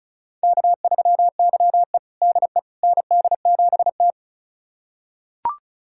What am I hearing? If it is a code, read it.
Morse code, heard as a keyed tone, about 36 wpm, 700 Hz. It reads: K3YE DE ND7T
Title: USA Callsign DE Callsign - 4 Characters - 35wpm